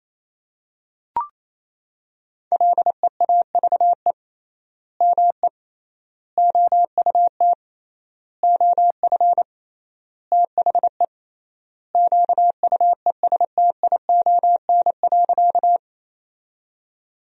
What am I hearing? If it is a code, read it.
Morse code, heard as a keyed tone, about 28 wpm, 700 Hz. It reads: LEAVE ME OUT OF THE QUESTION.
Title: 3x Word-Spacing - Sentences from Top 400 Words - 28wpm